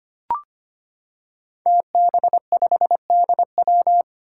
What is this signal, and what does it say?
Morse code, heard as a keyed tone, about 25 wpm, 700 Hz. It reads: TB5DW
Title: Five Challenging Letter-Number Combinations 25wpm